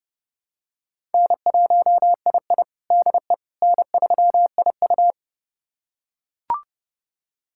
Morse code, heard as a keyed tone, about 30 wpm, 700 Hz. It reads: N1II DE N3IU
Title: USA Callsign DE Callsign - 4 Characters - 30wpm